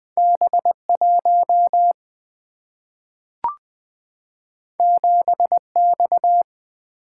Morse code, heard as a keyed tone, about 20 wpm, 700 Hz. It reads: B1 7X
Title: Two Letter-Number Combinations - 20wpm